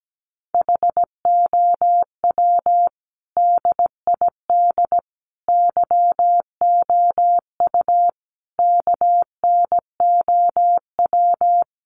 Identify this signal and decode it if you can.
Morse code, heard as a keyed tone, about 17 wpm, 700 Hz. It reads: HOW DID YOU KNOW
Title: Sentences from Top 100 Words 17wpm